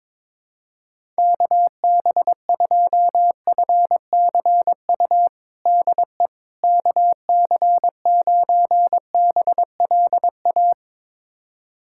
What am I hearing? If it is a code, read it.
Morse code, heard as a keyed tone, about 22 wpm, 700 Hz. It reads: KB2FCU DE KC9BLA
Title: USA Callsign DE Callsign 22wpm